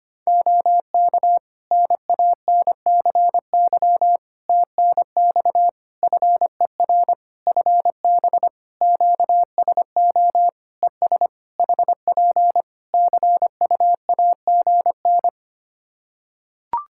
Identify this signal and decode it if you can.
Morse code, heard as a keyed tone, about 25 wpm, 700 Hz. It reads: OK NANCY TNX FER FB QSO ES HP CUAGN
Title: Practice QSO - Exchange 4 - Ending - 25wpm